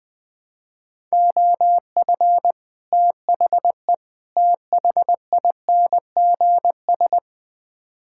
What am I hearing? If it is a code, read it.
Morse code, heard as a keyed tone, about 20 wpm, 700 Hz. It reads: OF THE THINGS
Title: Build A Phrase - 3-Word Phrases - 20wpm